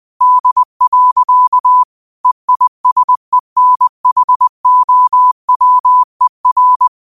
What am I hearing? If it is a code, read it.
Morse code, heard as a keyed tone, about 20 wpm, 1000 Hz. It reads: D. EISENHOWER
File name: qotdfiles / 20wpm